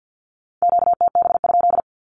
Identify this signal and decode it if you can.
Morse code, heard as a keyed tone, about 50 wpm, 700 Hz. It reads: QTB?
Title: Q-Codes with Spoken Meaning - 50wpm